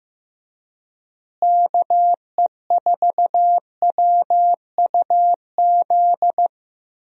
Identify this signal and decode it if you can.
Morse code, heard as a keyed tone, about 16 wpm, 700 Hz. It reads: KE4WUZ
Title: USA Call Signs - 6 Characters - Repeated 3x - 15wpm